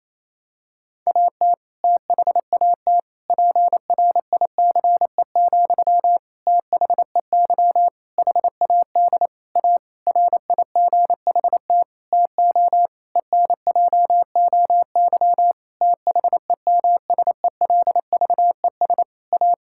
Morse code, heard as a keyed tone, about 28 wpm, 700 Hz. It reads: AT THAT PRICE, THEY HAD A RIGHT TO ENJOY THEMSELVES A
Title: Sentences from Top 1000 Words - Review Part 5 - 28wpm